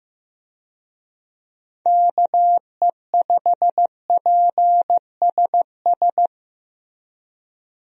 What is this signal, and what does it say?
Morse code, heard as a keyed tone, about 15 wpm, 700 Hz. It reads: KE5PSS
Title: USA Call Signs - 6 Characters - Repeated 3x - 15wpm